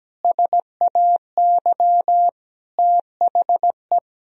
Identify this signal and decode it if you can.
Morse code, heard as a keyed tone, about 17 wpm, 700 Hz. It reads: SAY THE
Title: Sentences from Top 1000 Words - Review Part 3 - 17wpm